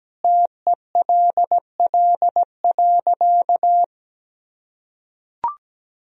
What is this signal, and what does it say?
Morse code, heard as a keyed tone, about 17 wpm, 700 Hz. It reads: TELL.
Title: Sentences from Top 1000 Words - Review Part 1 - 17wpm